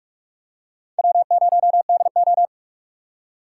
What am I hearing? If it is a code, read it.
Morse code, heard as a keyed tone, about 44 wpm, 700 Hz. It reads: W0DO
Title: USA Call Signs - 4 Characters - 3x Character Spacing - 45wpm